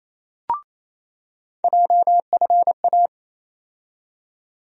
Morse code, heard as a keyed tone, about 28 wpm, 700 Hz. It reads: JFA
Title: USA Call Sign Postfixes - 3 Letters 28wpm